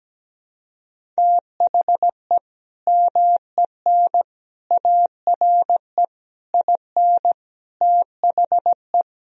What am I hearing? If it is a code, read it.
Morse code, heard as a keyed tone, about 17 wpm, 700 Hz. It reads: THE MEN ARE IN THE
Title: Sentences from Top 1000 Words - Review Part 1 - 17wpm